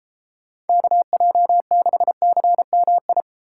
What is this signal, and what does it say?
Morse code, heard as a keyed tone, about 34 wpm, 700 Hz. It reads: KJ6CMI
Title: Speed Racing - USA Call Signs 28wpm